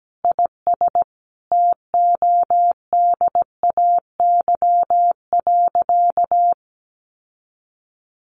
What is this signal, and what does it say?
Morse code, heard as a keyed tone, about 17 wpm, 700 Hz. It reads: IS TODAY.
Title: Sentences from Top 1500 Words 17wpm